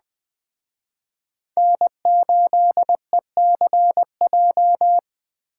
Morse code, heard as a keyed tone, about 20 wpm, 700 Hz. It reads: N8ECJ
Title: Speed Racing - USA 1x3 Call Signs 17wpm